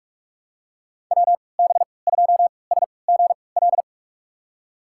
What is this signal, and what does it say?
Morse code, heard as a keyed tone, about 45 wpm, spaced as for 25 wpm, 700 Hz. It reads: WB2SGL
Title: USA Call Signs - 3x Character Spacing - 45wpm